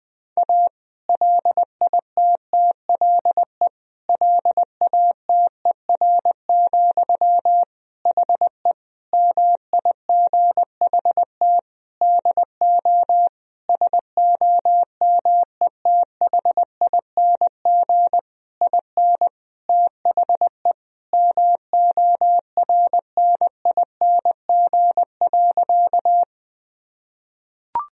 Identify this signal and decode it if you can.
Morse code, heard as a keyed tone, about 20 wpm, 700 Hz. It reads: A LITTLE LATER, HE MIGHT DO SOMETHING IN THE MORNING.
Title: Sentences from Top 1000 Words - Review Part 4 - 20wpm